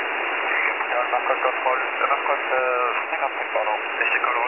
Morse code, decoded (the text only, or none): none